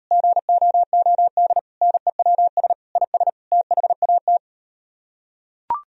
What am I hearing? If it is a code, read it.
Morse code, heard as a keyed tone, about 38 wpm, 700 Hz. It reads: GOOD NEWS IS THAT